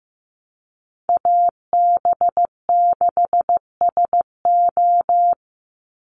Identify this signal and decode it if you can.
Morse code, heard as a keyed tone, about 15 wpm, 700 Hz. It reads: AB6SO
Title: NA SOTA Chaser Callsigns 15wpm